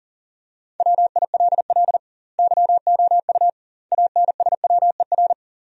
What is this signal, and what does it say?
Morse code, heard as a keyed tone, about 37 wpm, 700 Hz. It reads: WILL YOU ANSWER
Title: 2x Word-Spacing - Sentences from Top 400 Words - 40wpm